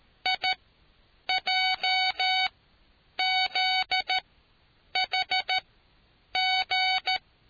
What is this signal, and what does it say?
Morse code, heard as a keyed tone, about 13 wpm, spaced as for 8 wpm, 750 Hz. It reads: IJZHG